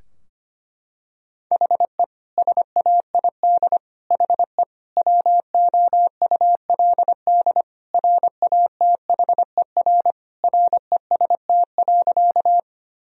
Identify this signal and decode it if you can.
Morse code, heard as a keyed tone, about 25 wpm, 700 Hz. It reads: HE SAID HE WOULD RATHER REST.
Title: Sentences from Top 1000 Words - Review Part 4 - 25wpm